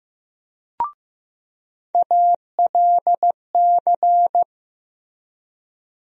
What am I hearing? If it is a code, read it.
Morse code, heard as a keyed tone, about 15 wpm, 700 Hz. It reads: ALC
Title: USA Call Sign Postfixes - 3 Letters 15wpm